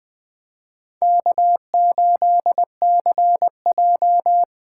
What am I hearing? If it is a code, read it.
Morse code, heard as a keyed tone, about 20 wpm, 700 Hz. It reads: K8CJ
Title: Speed Racing - USA Call Signs 17wpm